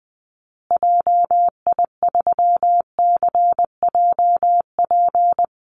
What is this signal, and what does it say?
Morse code, heard as a keyed tone, about 20 wpm, 700 Hz. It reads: JI3CJP
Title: Call Signs Repeated 2x - 20wpm